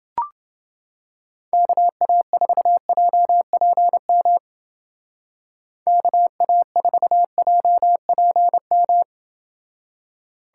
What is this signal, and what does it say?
Morse code, heard as a keyed tone, about 29 wpm, 700 Hz. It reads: KA4JPM KA4JPM